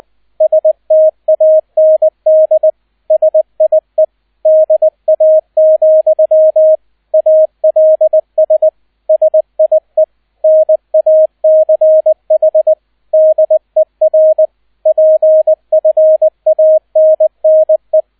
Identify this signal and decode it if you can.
Morse code, heard as a keyed tone, about 20 wpm, 630 Hz, with radio noise. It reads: STAND SIE DA, ALS SIE NACH DER PFANNE
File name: 002 Zweiter Streich 20wpm